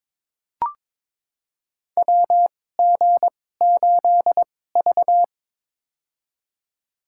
Morse code, heard as a keyed tone, about 22 wpm, spaced as for 16 wpm, 700 Hz. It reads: WG8V
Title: USA Call Signs - 2x Character Spacing - 22wpm